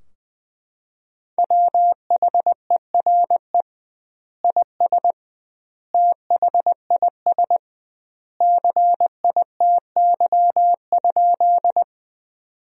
Morse code, heard as a keyed tone, about 20 wpm, 700 Hz. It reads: WHERE IS THIS CITY?
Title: 2x Word-Spacing - Sentences from Top 300 Words - 20wpm